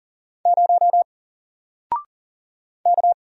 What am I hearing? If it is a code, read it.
Morse code, heard as a keyed tone, about 40 wpm, 700 Hz. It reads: 0 K
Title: Single Letter-Number Alt 40wpm